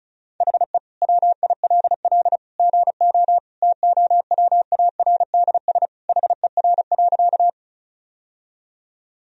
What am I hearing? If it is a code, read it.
Morse code, heard as a keyed tone, about 36 wpm, 700 Hz. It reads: HE WILL GO TOWARDS HER.
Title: Sentences from Top 1000 Words - Review Part 2 - 35wpm